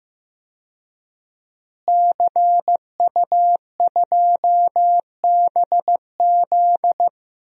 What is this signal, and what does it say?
Morse code, heard as a keyed tone, about 15 wpm, 700 Hz. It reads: CU2BZ
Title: DXCC-100 Call Signs - 5 Characters - Repeated 2x - 15wpm